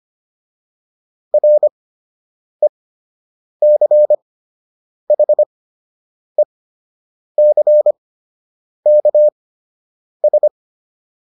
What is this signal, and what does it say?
Morse code, heard as a keyed tone, about 25 wpm, spaced as for 8 wpm, 600 Hz. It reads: RECHECKS